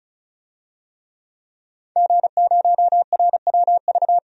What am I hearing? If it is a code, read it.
Morse code, heard as a keyed tone, about 36 wpm, 700 Hz. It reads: G0RWV
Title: DXCC-100 Call Signs - 5 Characters - 35wpm